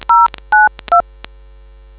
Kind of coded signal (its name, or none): DTMF